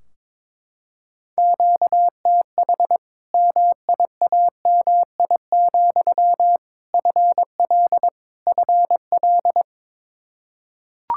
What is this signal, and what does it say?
Morse code, heard as a keyed tone, about 22 wpm, 700 Hz. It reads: QTH MIAMI, FL FL
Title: Practice QSO - Exchange 1 - Beginning - 22wpm